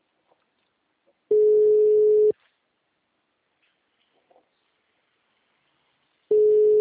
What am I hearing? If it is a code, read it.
Ringback tone (Europe, 425 Hz).